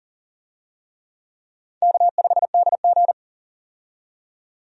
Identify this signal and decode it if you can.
Morse code, heard as a keyed tone, about 40 wpm, 690 Hz. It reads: K5DG
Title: USA Call Signs - 4 Characters - 40wpm